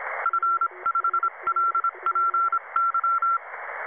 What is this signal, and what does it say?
Morse code, heard as a keyed tone, about 26 wpm, 1340 Hz. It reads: F5LPY
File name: Websdr Recording 2014 07 21t07 22 13z 7011 4khz